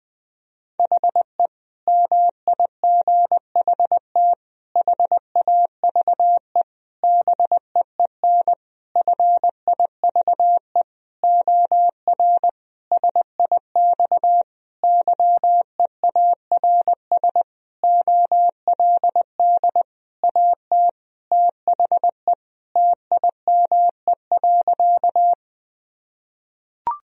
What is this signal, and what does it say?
Morse code, heard as a keyed tone, about 20 wpm, 700 Hz. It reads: HE MIGHT HAVE BEEN FIVE OR SIX YEARS OLD AT THE TIME.
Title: Sentences from Top 1000 Words - Review Part 2 - 20wpm